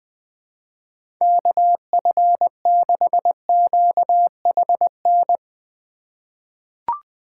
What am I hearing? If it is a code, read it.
Morse code, heard as a keyed tone, about 20 wpm, 700 Hz. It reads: KF6QHN
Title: USA Call Signs - 6 Characters - 3x Character Spacing - 20wpm